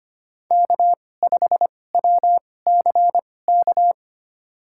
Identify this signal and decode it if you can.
Morse code, heard as a keyed tone, about 25 wpm, spaced as for 18 wpm, 700 Hz. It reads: K5WCK
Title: USA Call Signs - 5 Characters - 2x Character Spacing - 25wpm